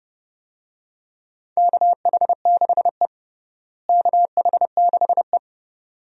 Morse code, heard as a keyed tone, about 30 wpm, 700 Hz. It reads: KH6E KH6E
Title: DXCC-100 Call Signs - 4 Characters - Repeated 2x - 30wpm